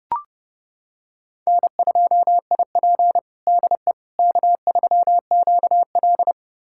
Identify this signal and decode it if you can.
Morse code, heard as a keyed tone, about 30 wpm, 700 Hz. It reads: N2IP DE K3QL
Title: USA Callsign DE Callsign - 4 Characters - 30wpm